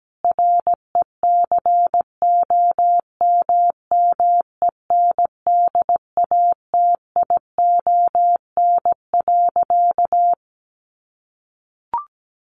Morse code, heard as a keyed tone, about 17 wpm, 700 Hz. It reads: RECOMMENDATION.
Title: Sentences from Top 1400 Words 17wpm